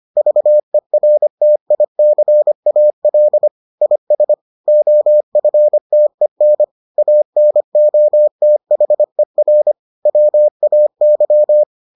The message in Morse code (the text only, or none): VERTICAL IS OFTEN ANOTHER WAY